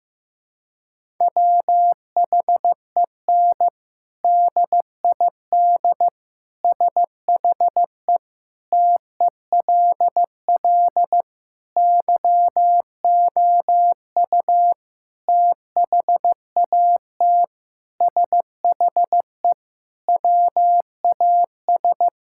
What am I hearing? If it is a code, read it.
Morse code, heard as a keyed tone, about 16 wpm, 700 Hz. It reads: WHEN DID SHE TELL YOU THAT SHE WAS
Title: Sentences from Top 1000 Words - Review Part 6 - 15wpm